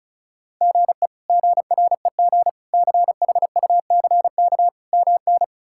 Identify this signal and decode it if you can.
Morse code, heard as a keyed tone, about 34 wpm, 700 Hz. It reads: GE GREG CHUCK MN
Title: K1USN SST QSO Practice - 35wpm